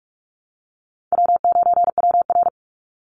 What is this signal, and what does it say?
Morse code, heard as a keyed tone, about 44 wpm, 700 Hz. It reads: W9WR
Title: Call Signs Repeated 3x - 45wpm